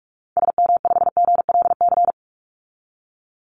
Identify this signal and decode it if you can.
Morse code, heard as a keyed tone, about 44 wpm, 700 Hz. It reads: SM5GLC
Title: Call Signs - 45wpm